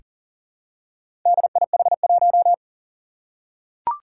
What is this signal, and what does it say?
Morse code, heard as a keyed tone, about 40 wpm, 700 Hz. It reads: DIH1
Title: Four Challenging Letter-Number Combinations 40wpm